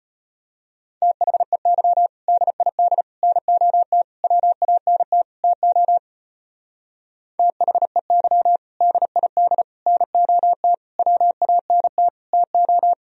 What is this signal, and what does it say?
Morse code, heard as a keyed tone, about 30 wpm, 700 Hz. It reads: THEY DID NOT WANT TO THEY DID NOT WANT TO
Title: Speed-Racing - Sets of 5 Words 25wpm